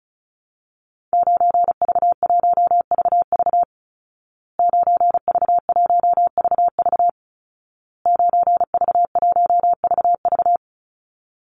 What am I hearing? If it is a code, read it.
Morse code, heard as a keyed tone, about 34 wpm, 700 Hz. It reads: 9V1VV 9V1VV 9V1VV
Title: Call Signs Repeated 2x - 35wpm